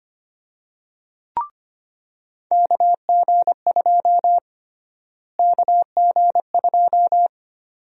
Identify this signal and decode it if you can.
Morse code, heard as a keyed tone, about 25 wpm, 700 Hz. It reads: KG2 KG2
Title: USA Call Sign Prefixes - Repeated 2x - 25wpm